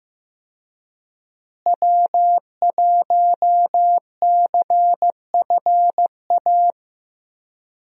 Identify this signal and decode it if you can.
Morse code, heard as a keyed tone, about 16 wpm, 700 Hz. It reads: W1CFA
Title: USA Call Signs - 5 Characters - Repeated 3x - 15wpm